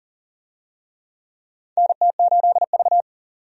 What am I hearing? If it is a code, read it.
Morse code, heard as a keyed tone, about 40 wpm, 700 Hz. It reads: NT8V
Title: USA Call Signs - 4 Characters - 40wpm